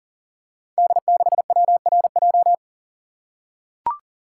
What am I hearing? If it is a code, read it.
Morse code, heard as a keyed tone, about 41 wpm, 700 Hz. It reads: D6WRJ
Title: Code Groups - 5 Letter:Numbers - 40wpm